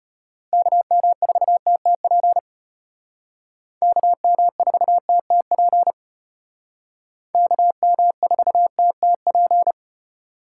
Morse code, heard as a keyed tone, about 32 wpm, 700 Hz. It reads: KM4TTP KM4TTP KM4TTP